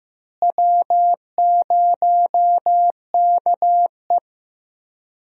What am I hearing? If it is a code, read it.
Morse code, heard as a keyed tone, about 15 wpm, 700 Hz. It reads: W0KE
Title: USA Call Signs - 4 Characters - Repeated 2x - 15wpm